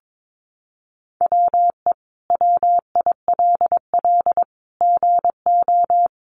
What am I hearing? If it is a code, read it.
Morse code, heard as a keyed tone, about 22 wpm, 700 Hz. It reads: WE WILL GO
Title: Sentences from Top 1500 Words 22wpm